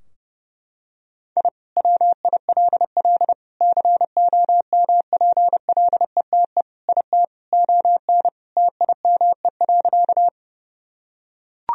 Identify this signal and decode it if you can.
Morse code, heard as a keyed tone, about 32 wpm, 700 Hz. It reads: I WILL COMPLETE IT ON TIME.
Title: Sentences from Top 1000 Words - Review Part 2 - 30wpm